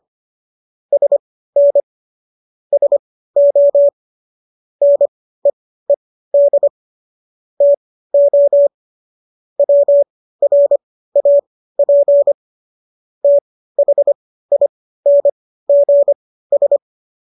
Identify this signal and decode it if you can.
Morse code, heard as a keyed tone, about 25 wpm, spaced as for 15 wpm, 570 Hz. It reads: SN SO NEED TO WRAP THINGS